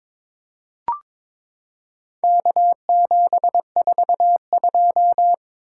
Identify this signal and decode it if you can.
Morse code, heard as a keyed tone, about 22 wpm, 700 Hz. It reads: K742
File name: practice / 22wpm / Four Letter-Number Combinations - 22wpm